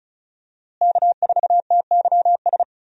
Morse code, heard as a keyed tone, about 36 wpm, 700 Hz. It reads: K4TYS
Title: USA Call Signs - 5 Characters - Repeated 2x - 35wpm